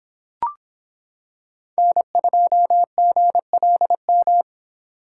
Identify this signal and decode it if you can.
Morse code, heard as a keyed tone, about 26 wpm, 700 Hz. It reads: N2GLM